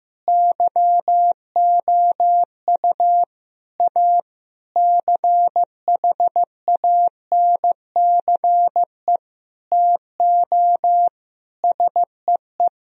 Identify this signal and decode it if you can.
Morse code, heard as a keyed tone, about 15 wpm, 700 Hz. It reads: YOU A CHANCE TO SEE
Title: Sentences from Top 1000 Words - Review Part 4 - 15wpm